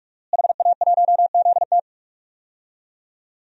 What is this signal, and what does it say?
Morse code, heard as a keyed tone, about 46 wpm, 700 Hz. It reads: HA1ZT
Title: DXCC-100 Call Signs - 5 Characters - Repeated 2x - 45wpm